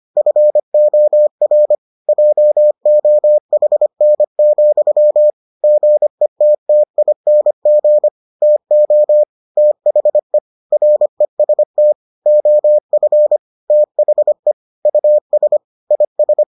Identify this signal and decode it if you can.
Morse code, heard as a keyed tone, about 25 wpm, 600 Hz. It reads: FOR JOHN, GETTING TO THE REST OF THE US IS